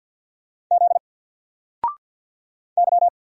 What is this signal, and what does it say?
Morse code, heard as a keyed tone, about 48 wpm, 700 Hz. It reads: Z Y
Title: Single Character (Letter:Number:Punctuation:Prosign) Alt 50wpm